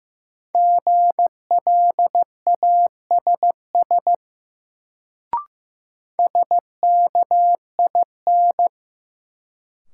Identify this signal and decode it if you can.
Morse code, heard as a keyed tone, about 15 wpm, 700 Hz. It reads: GLASS SKIN
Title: Sentences from Top 1300 Words - New Word Drill Alt - 15wpm